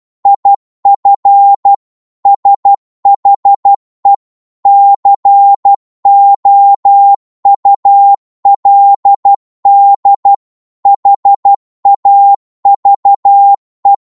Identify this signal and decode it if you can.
Morse code, heard as a keyed tone, about 12 wpm, 800 Hz. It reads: IF SHE COULD HAVE